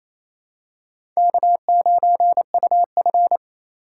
Morse code, heard as a keyed tone, about 27 wpm, 700 Hz. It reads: K9UF
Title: USA Call Signs - 4 Characters - 3x Character Spacing - 28wpm